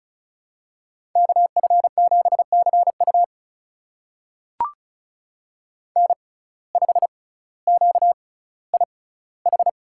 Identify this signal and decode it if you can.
Morse code, heard as a keyed tone, about 33 wpm, spaced as for 18 wpm, 700 Hz. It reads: KF7CU N5QIH
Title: USA Call Signs - 5 Characters - 6x Character Spacing - 35wpm